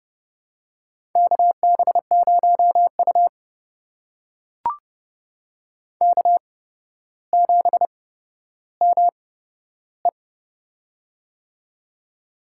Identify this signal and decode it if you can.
Morse code, heard as a keyed tone, about 30 wpm, spaced as for 13 wpm, 700 Hz. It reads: KB0U K7ME
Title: USA Call Signs - 4 Characters - 8x Character Spacing - 30wpm